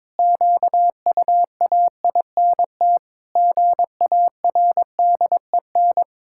Morse code, heard as a keyed tone, about 22 wpm, 700 Hz. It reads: QUAINT GARDEN
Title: Pangram Challenge v2 - 22wpm